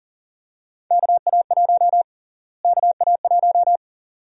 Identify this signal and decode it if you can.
Morse code, heard as a keyed tone, about 40 wpm, 700 Hz. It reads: KA1 KA1
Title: USA Call Sign Prefixes - Repeated 2x - 40wpm